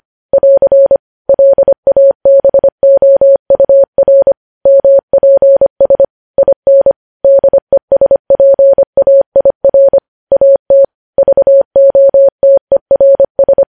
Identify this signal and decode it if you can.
Morse code, heard as a keyed tone, about 25 wpm, 550 Hz. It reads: <AR> LABOUR MPS IN DESPAIR AT VOTERS
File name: podcast / News headlines-1x-25x25